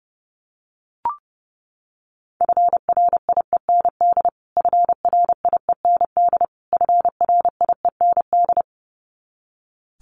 Morse code, heard as a keyed tone, about 30 wpm, 700 Hz. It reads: FRIEND FRIEND FRIEND
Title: Top 500 Words Repeated 30wpm